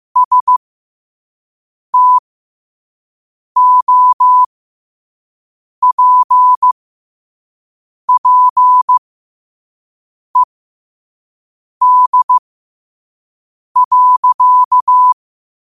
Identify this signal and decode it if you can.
Morse code, heard as a keyed tone, about 15 wpm, spaced as for 5 wpm, 1000 Hz. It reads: STOPPED.
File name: qotdfiles / 5wpm